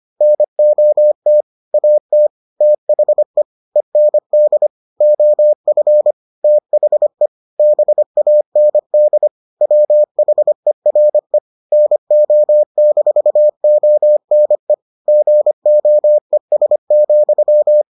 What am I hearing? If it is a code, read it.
Morse code, heard as a keyed tone, about 25 wpm, 600 Hz. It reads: NOT AT THE END OF THE BAND WHERE NO-ONE GOES,